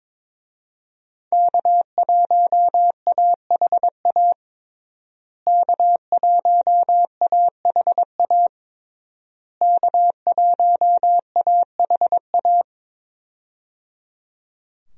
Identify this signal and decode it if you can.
Morse code, heard as a keyed tone, about 22 wpm, 700 Hz. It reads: K1AHA K1AHA K1AHA
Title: USA Call Signs - 5 Characters - Repeated 3x - 22wpm